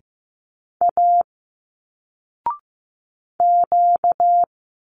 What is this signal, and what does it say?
Morse code, heard as a keyed tone, about 15 wpm, 700 Hz. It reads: A Q
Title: Single Letters 15wpm